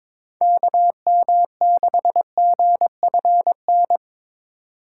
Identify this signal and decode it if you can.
Morse code, heard as a keyed tone, about 22 wpm, 700 Hz. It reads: KM6GFN
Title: USA Call Signs - 6 Characters - 4x Character Spacing - 22wpm